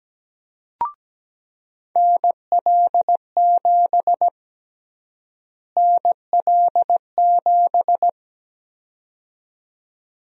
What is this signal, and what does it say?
Morse code, heard as a keyed tone, about 17 wpm, 700 Hz. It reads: NL7 NL7
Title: USA Call Sign Prefixes - Repeated 2x - 17wpm